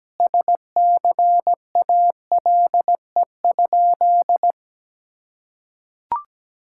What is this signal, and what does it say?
Morse code, heard as a keyed tone, about 17 wpm, 700 Hz. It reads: SCALE?
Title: Sentences from Top 1000 Words - Review Part 5 - 17wpm